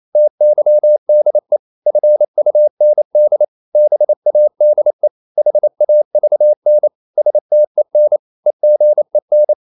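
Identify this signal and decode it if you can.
Morse code, heard as a keyed tone, about 28 wpm, 600 Hz. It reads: TYDE FUND BADE HAVN STEN EGEN
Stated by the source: Danish short words sent in Morse at 28 wpm, from CW med Gnister.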